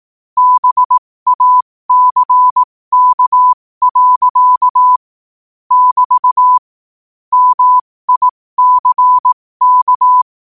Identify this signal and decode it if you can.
Morse code, heard as a keyed tone, about 18 wpm, spaced as for 15 wpm, 1000 Hz. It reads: BACK. <BT> MICK
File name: qotdfiles / 15wpm